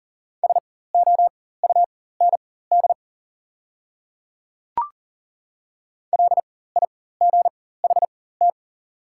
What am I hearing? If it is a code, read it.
Morse code, heard as a keyed tone, about 40 wpm, spaced as for 19 wpm, 700 Hz. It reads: SOUND LIGHT
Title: 4x Character Spacing - Top 200 Words - 40wpm